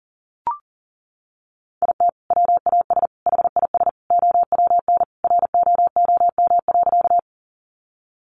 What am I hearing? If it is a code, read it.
Morse code, heard as a keyed tone, about 40 wpm, 700 Hz. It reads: IT WAS HIS OWN ROOM.
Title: Sentences from Top 300 Words 40wpm